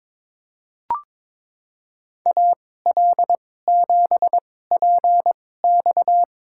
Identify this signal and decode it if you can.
Morse code, heard as a keyed tone, about 21 wpm, spaced as for 16 wpm, 700 Hz. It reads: AL7PX
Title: USA Call Signs - 2x Character Spacing - 22wpm